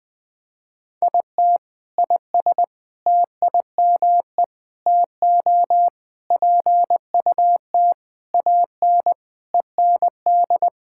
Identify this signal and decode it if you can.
Morse code, heard as a keyed tone, about 20 wpm, 700 Hz. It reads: IT IS TIME TO PUT AN END
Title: Sentences with 6-Word Phrases - 20wpm